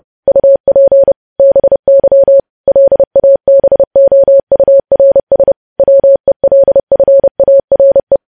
Morse code, heard as a keyed tone, about 30 wpm, 550 Hz. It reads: UP BY LABOURS WELFARE
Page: news headlines in morse code at 30 wpm